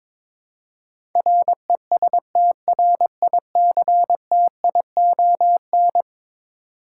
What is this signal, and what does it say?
Morse code, heard as a keyed tone, about 21 wpm, 700 Hz. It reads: RESTRICTION
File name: practice / 22wpm / Eleven Letter Words - 22wpm